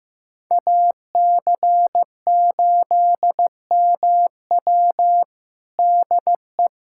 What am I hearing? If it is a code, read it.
Morse code, heard as a keyed tone, about 15 wpm, 700 Hz. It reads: AC8MW DE
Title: USA Callsign DE Callsign - 5 Characters - 15wpm